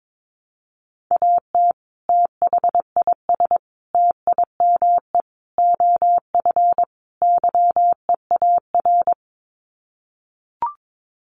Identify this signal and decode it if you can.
Morse code, heard as a keyed tone, about 22 wpm, 700 Hz. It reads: AT THIS TIME OF YEAR
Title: Sets of 5 Words Encore2 22wpm